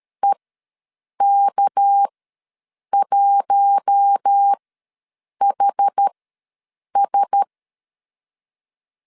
Morse code, heard as a keyed tone, about 13 wpm, spaced as for 7 wpm, 800 Hz. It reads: EK1HS
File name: LEZIONI / 6 PKX0 / 6 PK ws 12wpm 3116